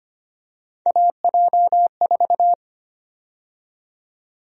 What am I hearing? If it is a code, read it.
Morse code, heard as a keyed tone, about 25 wpm, 700 Hz. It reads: AJ4
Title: USA Call Sign Prefixes - 25wpm